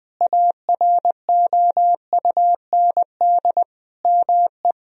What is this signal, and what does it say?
Morse code, heard as a keyed tone, about 20 wpm, 700 Hz. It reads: AROUND ME
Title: Sentences from Top 1000 Words - Review Part 3 - 20wpm